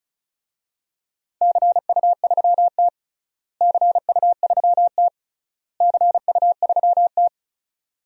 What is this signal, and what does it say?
Morse code, heard as a keyed tone, about 36 wpm, 700 Hz. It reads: CU3T CU3T CU3T
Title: DXCC-100 Call Signs - 4 Characters - Repeated 3x - 35wpm